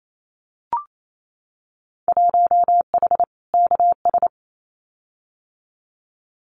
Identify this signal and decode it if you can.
Morse code, heard as a keyed tone, about 28 wpm, 700 Hz. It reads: 1H KS
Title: Winter Field Day Exchanges 28wpm